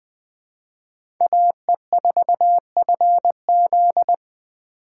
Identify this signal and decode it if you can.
Morse code, heard as a keyed tone, about 20 wpm, 700 Hz. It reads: AE4FZ
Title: 2023 POTA Hunter Call Signs - 20wpm